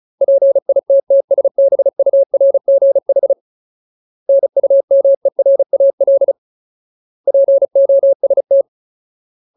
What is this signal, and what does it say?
Morse code, heard as a keyed tone, about 34 wpm, 550 Hz. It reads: PITTSBURGH NUMERAL POST
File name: Ultimate Word List / 35WPM Ultimate